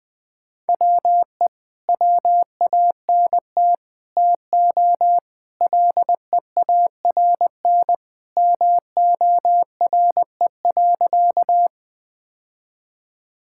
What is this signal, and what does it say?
Morse code, heard as a keyed tone, about 20 wpm, 700 Hz. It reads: WE WANT TO LEARN MORE.
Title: Sentences with 3-Word Phrases - Spoken Beginning - 20wpm